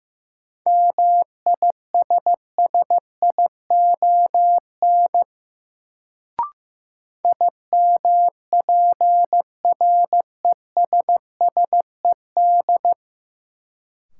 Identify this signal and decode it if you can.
Morse code, heard as a keyed tone, about 15 wpm, 700 Hz. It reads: MISSION IMPRESSED
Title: Sentences from Top 1600 Words - New Word Drill Alt - 15wpm